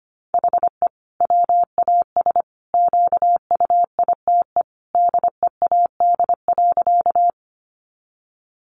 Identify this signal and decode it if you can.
Morse code, heard as a keyed tone, about 25 wpm, 700 Hz. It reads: HE WAS QUITE DEAD.
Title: Sentences from Top 1500 Words 25wpm